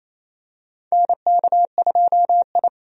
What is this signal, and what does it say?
Morse code, heard as a keyed tone, about 28 wpm, 700 Hz. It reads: NK2I
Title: USA Call Signs - 4 Characters - 28wpm